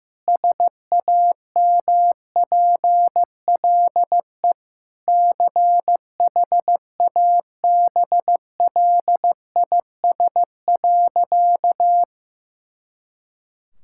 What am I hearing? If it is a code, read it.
Morse code, heard as a keyed tone, about 15 wpm, 700 Hz. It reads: SAMPLE CHABLIS.
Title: Pangram Challenge v2 - 15wpm